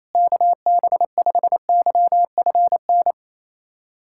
Morse code, heard as a keyed tone, about 28 wpm, 690 Hz. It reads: KB5YFN
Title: SOTA QSO Practice 28wpm